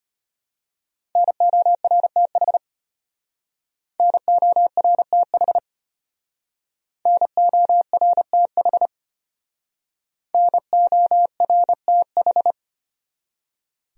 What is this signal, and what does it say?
Morse code, heard as a keyed tone, about 32 wpm, 700 Hz. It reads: NORTH NORTH NORTH NORTH